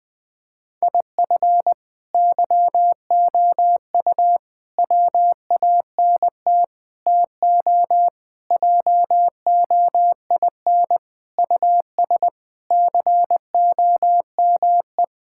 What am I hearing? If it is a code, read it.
Morse code, heard as a keyed tone, about 20 wpm, 700 Hz. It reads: IF YOU WANT TO JOIN US COME
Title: Sentences with 4-Word Phrases - 20wpm